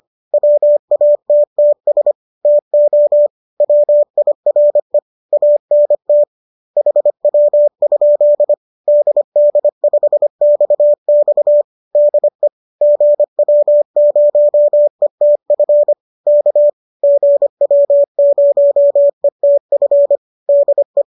Morse code, heard as a keyed tone, about 25 wpm, 590 Hz. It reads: WATTS TO WIRE ANT HW? DD5XX DE GW0ETF K GW0ETF DE